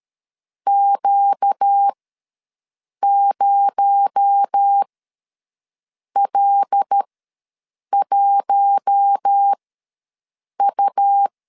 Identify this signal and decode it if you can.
Morse code, heard as a keyed tone, about 13 wpm, spaced as for 6 wpm, 800 Hz. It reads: Q0L1U
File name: LEZIONI / 6 PKX0 / 6 PKX0 ws 12wpm 3448